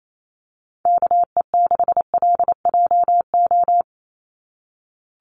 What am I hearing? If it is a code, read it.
Morse code, heard as a keyed tone, about 28 wpm, 700 Hz. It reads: KE6LJO
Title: USA Call Signs 28wpm